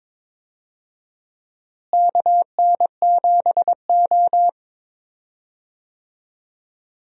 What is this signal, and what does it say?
Morse code, heard as a keyed tone, about 22 wpm, 700 Hz. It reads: KN7O
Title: USA Call Signs - 4 Characters - Repeated 2x - 22wpm